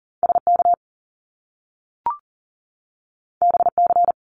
Morse code, heard as a keyed tone, about 40 wpm, 700 Hz. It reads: SK BC
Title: Canadian Provinces and Territories 40wpm